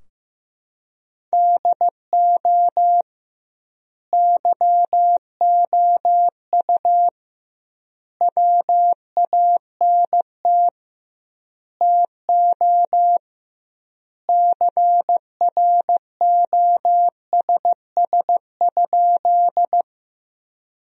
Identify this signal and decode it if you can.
Morse code, heard as a keyed tone, about 15 wpm, 700 Hz. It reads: DO YOU WANT TO CROSS?
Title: 2x Word-Spacing - Sentences from Top 300 Words - 15wpm